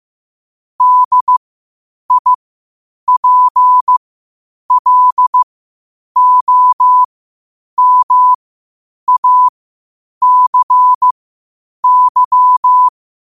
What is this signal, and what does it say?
Morse code, heard as a keyed tone, about 15 wpm, spaced as for 8 wpm, 1000 Hz. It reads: DIPLOMACY
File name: qotdfiles / 8wpm